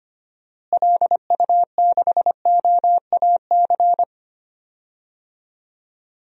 Morse code, heard as a keyed tone, about 25 wpm, 700 Hz. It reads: LU6OAC
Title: DXCC-100 Call Signs - 6 Characters - Repeated 2x - 25wpm